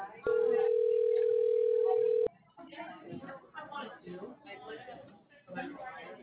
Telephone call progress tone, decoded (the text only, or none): ringback tone